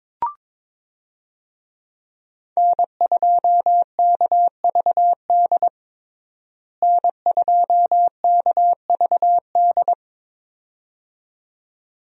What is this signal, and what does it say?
Morse code, heard as a keyed tone, about 22 wpm, 700 Hz. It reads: N2KVD N2KVD
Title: USA Call Signs - 5 Characters - Repeated 2x - 22wpm